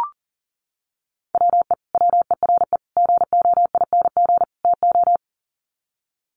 Morse code, heard as a keyed tone, about 40 wpm, 700 Hz. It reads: WE WERE GOING TO
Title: Sets of 4 Words 40wpm